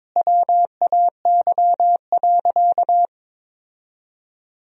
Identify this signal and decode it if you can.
Morse code, heard as a keyed tone, about 21 wpm, 700 Hz. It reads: WAY.
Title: 4x Word-Spacing - Sentences from Top 400 Words - 22wpm